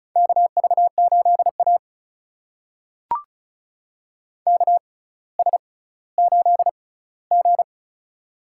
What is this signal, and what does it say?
Morse code, heard as a keyed tone, about 36 wpm, spaced as for 18 wpm, 700 Hz. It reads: KV8A KS8G
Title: USA Call Signs - 4 Characters - 6x Character Spacing - 35wpm